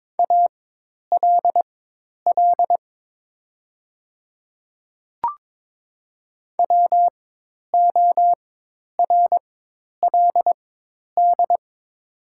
Morse code, heard as a keyed tone, about 22 wpm, spaced as for 10 wpm, 700 Hz. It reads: ALL WORLD
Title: 4x Character Spacing - Top 200 Words - 22wpm